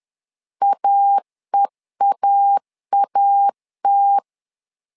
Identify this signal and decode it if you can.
Morse code, heard as a keyed tone, about 10 wpm, 800 Hz. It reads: AEAAT
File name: LEZIONI / 2 UHDN / 2 UH 10wpm 1714